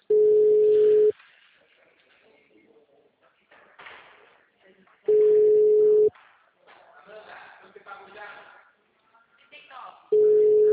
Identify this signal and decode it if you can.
Ringback tone (Europe, 425 Hz), with speech.